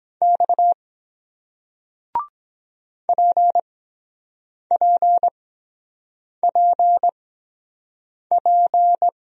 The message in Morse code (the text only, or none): X PPPP